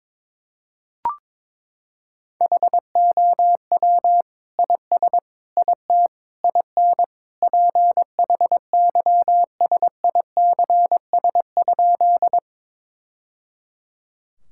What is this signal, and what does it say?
Morse code, heard as a keyed tone, about 22 wpm, 700 Hz. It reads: HOW IS IT IN PHYSICS?
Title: Sentences from Top 1000 Words - Review Part 5 - 22wpm